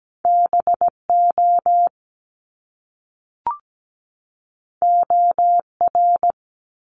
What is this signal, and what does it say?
Morse code, heard as a keyed tone, about 17 wpm, 700 Hz. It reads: BO OR
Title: Top Two Letter Combinations in Words 17wpm